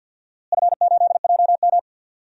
Morse code, heard as a keyed tone, about 47 wpm, 700 Hz. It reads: F8JM
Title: DXCC-100 Call Signs - 4 Characters - Repeated 2x - 50wpm